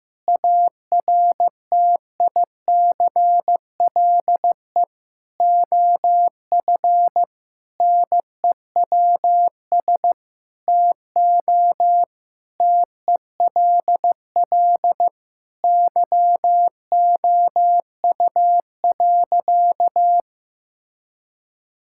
Morse code, heard as a keyed tone, about 16 wpm, 700 Hz. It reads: ARTICLE OF NEWS TO TELL YOU.
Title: Sentences from Top 1000 Words - Review Part 5 - 15wpm